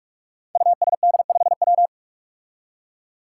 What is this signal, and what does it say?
Morse code, heard as a keyed tone, about 44 wpm, 700 Hz. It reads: USD5W
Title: Five Challenging Letter-Number Combinations 45wpm